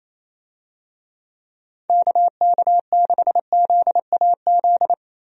Morse code, heard as a keyed tone, about 28 wpm, 700 Hz. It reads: KK6ZAZ
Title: USA Call Signs - 6 Characters - 28wpm